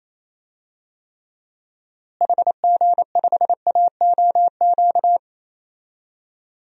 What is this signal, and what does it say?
Morse code, heard as a keyed tone, about 28 wpm, 700 Hz. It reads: HG5AOQ
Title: DXCC-100 Call Signs - 6 Characters - Repeated 3x - 28wpm